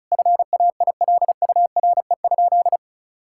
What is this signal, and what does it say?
Morse code, heard as a keyed tone, about 36 wpm, 700 Hz. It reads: FAILURE?
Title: Sentences from Top 1000 Words - Review Part 6 - 35wpm